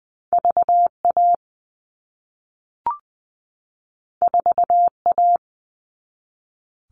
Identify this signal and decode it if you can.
Morse code, heard as a keyed tone, about 20 wpm, 700 Hz. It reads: VA 4A
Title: All Cut Numbers - 2-Digit 20wpm